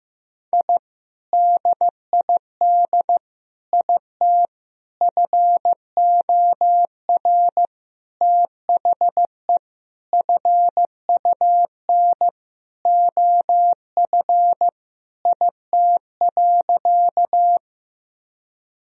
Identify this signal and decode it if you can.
Morse code, heard as a keyed tone, about 16 wpm, 700 Hz. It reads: I DID IT FOR THE FUN OF IT.
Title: Sentences from Top 1000 Words - Review Part 4 - 15wpm